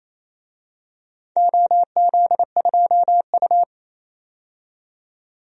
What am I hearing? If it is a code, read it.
Morse code, heard as a keyed tone, about 28 wpm, 700 Hz. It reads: OZ2U
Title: DXCC-100 Call Signs - 4 Characters - 28wpm